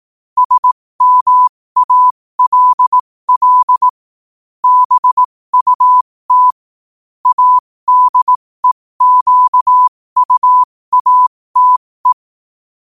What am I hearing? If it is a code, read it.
Morse code, heard as a keyed tone, about 18 wpm, spaced as for 15 wpm, 1000 Hz. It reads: SMALL BUT ADEQUATE
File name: qotdfiles / 15wpm